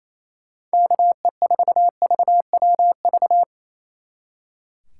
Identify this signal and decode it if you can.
Morse code, heard as a keyed tone, about 37 wpm, 700 Hz. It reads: KE4VWV